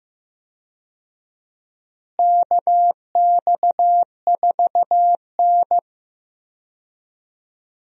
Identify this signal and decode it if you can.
Morse code, heard as a keyed tone, about 16 wpm, 700 Hz. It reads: KX4N
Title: USA Call Signs - 4 Characters - Repeated 2x - 15wpm